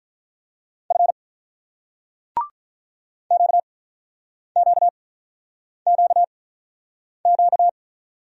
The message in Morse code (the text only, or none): F QQQQ